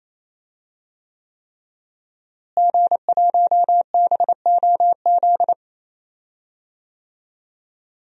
Morse code, heard as a keyed tone, about 28 wpm, 700 Hz. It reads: G1BOZ
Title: DXCC-100 Call Signs - 5 Characters - Repeated 3x - 28wpm